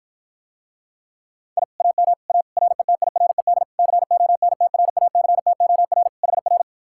Morse code, heard as a keyed tone, about 53 wpm, 700 Hz. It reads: I AM A RETIRED CONTRACTOR HR